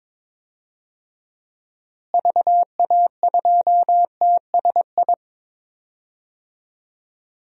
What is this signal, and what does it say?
Morse code, heard as a keyed tone, about 21 wpm, 700 Hz. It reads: VA2TSI
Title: DXCC-100 Call Signs - 6 Characters - Repeated 2x - 22wpm